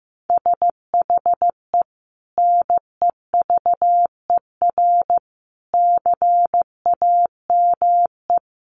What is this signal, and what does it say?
Morse code, heard as a keyed tone, about 14 wpm, 700 Hz. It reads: SHE NEVER CAME
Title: Sentences from Top 300 Words 15wpm